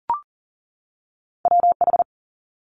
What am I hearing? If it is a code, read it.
Morse code, heard as a keyed tone, about 41 wpm, 700 Hz. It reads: WH
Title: Multi-letter Phonemes 40wpm